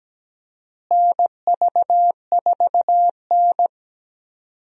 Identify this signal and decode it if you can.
Morse code, heard as a keyed tone, about 17 wpm, 700 Hz. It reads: NV4N
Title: USA Call Signs - 4 Characters - Repeated 2x - 17wpm